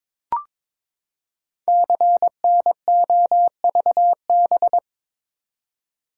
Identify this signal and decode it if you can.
Morse code, heard as a keyed tone, about 22 wpm, 700 Hz. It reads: CNOVB
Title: Code Groups - 5 Letters - 22wpm